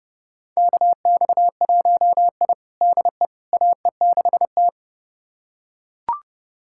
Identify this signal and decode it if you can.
Morse code, heard as a keyed tone, about 29 wpm, 700 Hz. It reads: KX1I DE AE6T
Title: USA Callsign DE Callsign - 4 Characters - 30wpm